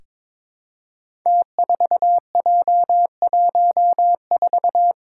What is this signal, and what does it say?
Morse code, heard as a keyed tone, about 22 wpm, 700 Hz. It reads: T4J14
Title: Five Challenging Letter-Number Combinations 22wpm